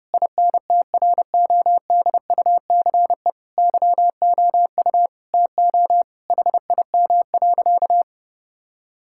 Morse code, heard as a keyed tone, about 30 wpm, 700 Hz. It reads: INTRODUCE YOU TO HIM.
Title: Sentences from Top 1600 Words 30wpm